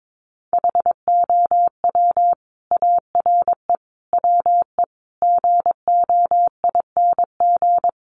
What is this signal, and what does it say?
Morse code, heard as a keyed tone, about 22 wpm, 700 Hz. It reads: HOW ARE WE GOING
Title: Sets of 5 Words Encore 22wpm